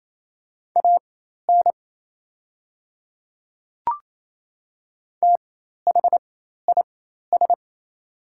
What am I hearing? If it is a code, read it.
Morse code, heard as a keyed tone, about 28 wpm, spaced as for 13 wpm, 700 Hz. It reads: AN THIS
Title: 4x Character Spacing - Top 100 Words - 28wpm